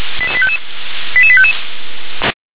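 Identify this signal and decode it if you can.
Five-tone selective call (ZVEI), heard as quick runs of five tones, tones of about 70 ms. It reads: 88855
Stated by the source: ZVEI five-tone selective call received by SDR.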